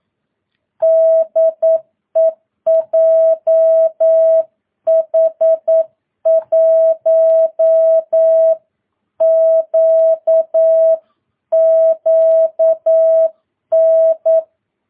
Morse code, heard as a keyed tone, about 9 wpm, 650 Hz. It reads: DE JH1QQN
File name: jh8cbh / 2022cw